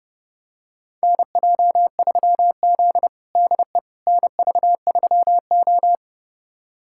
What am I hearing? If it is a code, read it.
Morse code, heard as a keyed tone, about 32 wpm, 700 Hz. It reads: NJ3Z DE NV3O
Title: USA Callsign DE Callsign - 4 Characters - 30wpm